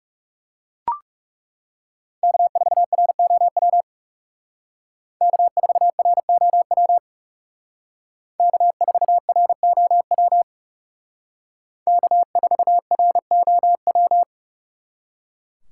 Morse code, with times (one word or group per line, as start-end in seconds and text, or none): 2.2-3.8 K4ROW
5.2-7.0 K4ROW
8.4-10.4 K4ROW
11.9-14.2 K4ROW